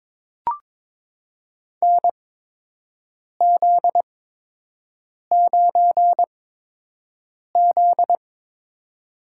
Morse code, heard as a keyed tone, about 22 wpm, spaced as for 6 wpm, 700 Hz. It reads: NZ9Z
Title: USA Call Signs - 4 Characters - 8x Character Spacing - 22wpm